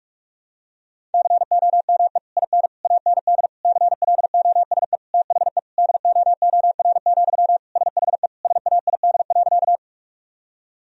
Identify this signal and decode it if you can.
Morse code, heard as a keyed tone, about 41 wpm, 700 Hz. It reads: COME IN AND CLOSE THE DOOR, SHE SAID.
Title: Sentences from Top 1000 Words - Review Part 1 - 45wpm